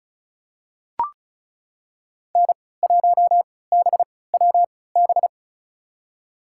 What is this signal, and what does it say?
Morse code, heard as a keyed tone, about 36 wpm, spaced as for 20 wpm, 700 Hz. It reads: N1BWB
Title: USA Call Signs - 5 Characters - 3x Character Spacing - 35wpm